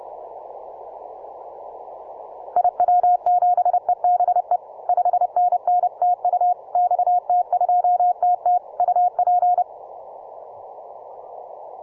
Morse code, heard as a keyed tone, about 29 wpm, 720 Hz. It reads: IW7EBE 5NN TU XT2TT UP